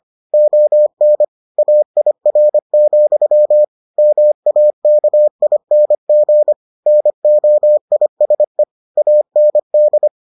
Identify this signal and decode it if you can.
Morse code, heard as a keyed tone, about 25 wpm, 600 Hz. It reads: ON AIR, MAKING NOISE AND